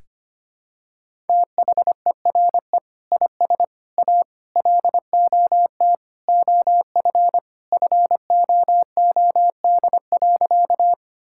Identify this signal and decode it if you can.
Morse code, heard as a keyed tone, about 25 wpm, 700 Hz. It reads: THERE IS A LOT OF FOOD.
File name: practice / 25wpm / Sentences with 3-Word Phrases - 25wpm